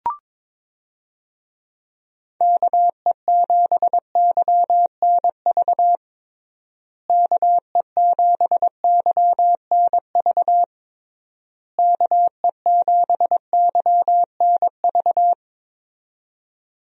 Morse code, heard as a keyed tone, about 22 wpm, 700 Hz. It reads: KE7YNV KE7YNV KE7YNV
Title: USA Call Signs - 6 Characters - Repeated 3x - 22wpm